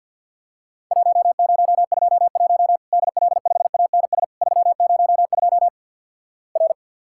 Morse code, heard as a keyed tone, about 50 wpm, 700 Hz. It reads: 1021 DL5ANS 302 R
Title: Worked All Europe DX Contest - QTC Exchange - 50wpm